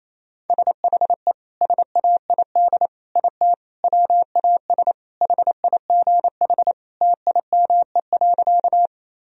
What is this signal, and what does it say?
Morse code, heard as a keyed tone, about 28 wpm, 700 Hz. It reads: SHE SAID IT WAS HIGH TIME.
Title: Sentences from Top 1000 Words - Review Part 1 - 28wpm